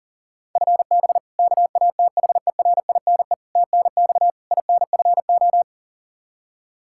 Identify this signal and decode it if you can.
Morse code, heard as a keyed tone, about 40 wpm, 700 Hz. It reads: FB KATHERINE TNX INFO
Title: Practice QSO - Exchange 3 - More Info - 40wpm